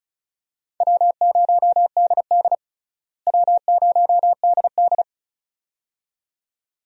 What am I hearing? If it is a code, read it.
Morse code, heard as a keyed tone, about 36 wpm, 700 Hz. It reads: W0DD W0DD
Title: USA Call Signs - 4 Characters - Repeated 2x - 35wpm